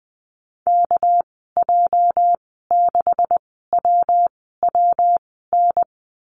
Morse code, heard as a keyed tone, about 20 wpm, spaced as for 15 wpm, 700 Hz. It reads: KJ6WWN
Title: 213 Lesson 34g - Number 8 - Call Signs 20wpm